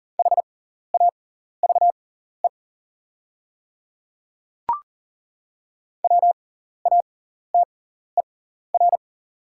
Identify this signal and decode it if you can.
Morse code, heard as a keyed tone, about 41 wpm, spaced as for 14 wpm, 690 Hz. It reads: HAVE WATER
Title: 6x Character Spacing - Top 100 Words - 40wpm